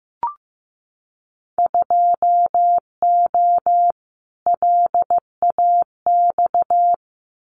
Morse code, heard as a keyed tone, about 15 wpm, 700 Hz. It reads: 2O LAX
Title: Winter Field Day Exchanges 15wpm